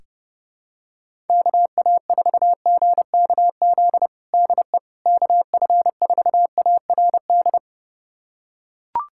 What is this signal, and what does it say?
Morse code, heard as a keyed tone, about 29 wpm, 700 Hz. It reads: KA4GKZ DE KF4ARD
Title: USA Callsign DE Callsign - 6 Characters - 30wpm